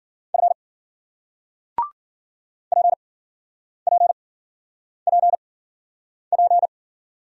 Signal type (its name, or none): Morse code